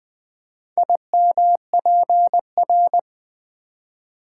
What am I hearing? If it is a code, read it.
Morse code, heard as a keyed tone, about 20 wpm, 700 Hz. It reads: IMPR
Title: Build A Word - 4 Letters At A Time - 10-Letter Words - 20wpm